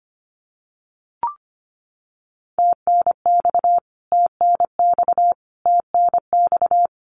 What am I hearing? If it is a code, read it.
Morse code, heard as a keyed tone, about 25 wpm, 700 Hz. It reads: TNX TNX TNX
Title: Abbreviations Repeated 3x - 25wpm